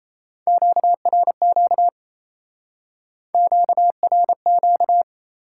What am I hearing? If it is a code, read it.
Morse code, heard as a keyed tone, about 36 wpm, 700 Hz. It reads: QRQ QRQ